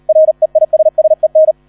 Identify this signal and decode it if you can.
Morse code, heard as a keyed tone, about 38 wpm, 630 Hz, with radio noise. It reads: REISSEN